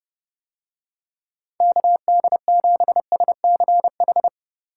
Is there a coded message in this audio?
Yes: Morse code from 1.6 to 4.3 s, heard as a keyed tone, about 30 wpm, 700 Hz. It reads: KD7SCH